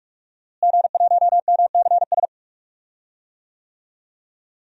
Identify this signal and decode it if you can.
Morse code, heard as a keyed tone, about 43 wpm, 700 Hz. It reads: G1MCS
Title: DXCC-100 Call Signs - 5 Characters - Repeated 3x - 45wpm